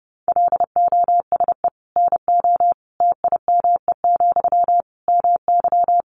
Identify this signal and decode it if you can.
Morse code, heard as a keyed tone, about 28 wpm, 700 Hz. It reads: LOSE NO TIME, MY
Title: Sentences from Top 1100 Words 30wpm